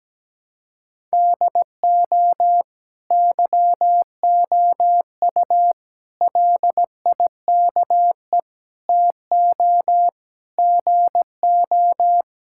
Morse code, heard as a keyed tone, about 17 wpm, 700 Hz. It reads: DO YOU LIKE TO GO
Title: Sentences from Top 1000 Words - Review Part 5 - 17wpm